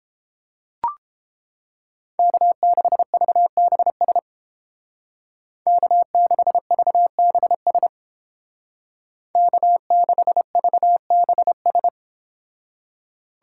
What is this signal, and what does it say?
Morse code, heard as a keyed tone, about 31 wpm, 700 Hz. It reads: K6VBS K6VBS K6VBS